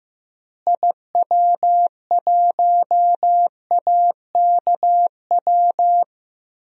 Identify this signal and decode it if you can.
Morse code, heard as a keyed tone, about 15 wpm, 700 Hz. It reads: IW1AKW
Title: DXCC-100 Call Signs - 6 Characters - Repeated 3x - 15wpm